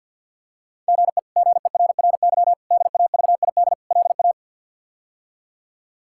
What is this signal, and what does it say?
Morse code, heard as a keyed tone, about 47 wpm, 700 Hz. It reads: GE GERRY DAVID LA
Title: K1USN SST QSO Practice - 50wpm